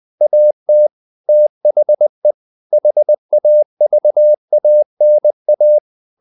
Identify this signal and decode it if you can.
Morse code, heard as a keyed tone, about 20 wpm, 600 Hz. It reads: AT THE HAVANA
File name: the old man and the sea part 2 20wpm